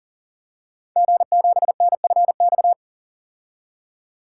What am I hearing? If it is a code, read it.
Morse code, heard as a keyed tone, about 40 wpm, 690 Hz. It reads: G7NFX
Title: Code Groups - 5 Letter:Numbers - 40wpm